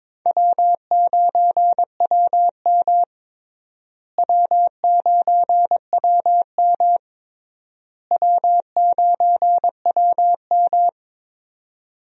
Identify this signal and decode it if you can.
Morse code, heard as a keyed tone, about 22 wpm, 700 Hz. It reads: W9WM W9WM W9WM
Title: USA Call Signs - 4 Characters - Repeated 3x - 22wpm